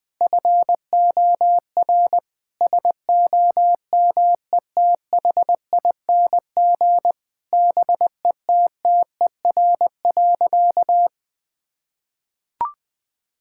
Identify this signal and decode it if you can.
Morse code, heard as a keyed tone, about 20 wpm, 700 Hz. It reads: FOR SOMETHING BETTER.
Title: Sentences from Top 1000 Words - Review Part 6 - 20wpm